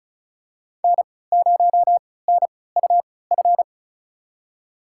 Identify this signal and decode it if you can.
Morse code, heard as a keyed tone, about 34 wpm, spaced as for 20 wpm, 700 Hz. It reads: N0NUF
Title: USA Call Signs - 5 Characters - 3x Character Spacing - 35wpm